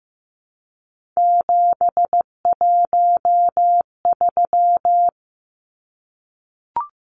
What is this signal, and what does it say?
Morse code, heard as a keyed tone, about 15 wpm, 700 Hz. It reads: 713
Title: Numbers - 3-Digit 15wpm